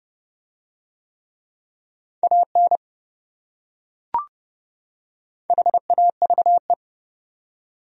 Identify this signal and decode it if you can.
Morse code, heard as a keyed tone, about 28 wpm, 700 Hz. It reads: AN HAVE